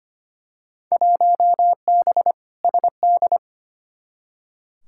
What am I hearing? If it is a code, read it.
Morse code, heard as a keyed tone, about 25 wpm, 700 Hz. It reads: 1B SD
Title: Field Day Exchanges Encore - 25wpm